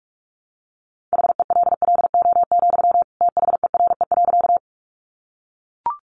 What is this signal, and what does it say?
Morse code, heard as a keyed tone, about 44 wpm, 700 Hz. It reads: HELLO, THERE.
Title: Sentences from Top 1100 Words 45wpm